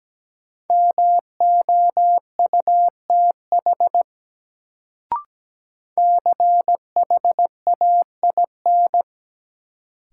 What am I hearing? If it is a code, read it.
Morse code, heard as a keyed tone, about 17 wpm, 700 Hz. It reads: MOUTH CHAIN
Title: Sentences from Top 1500 Words - New Word Drill - 17wpm